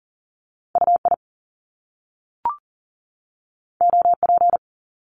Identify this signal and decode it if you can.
Morse code, heard as a keyed tone, about 40 wpm, 700 Hz. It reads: UI OP
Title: Top Two Letter Combinations in Words 40wpm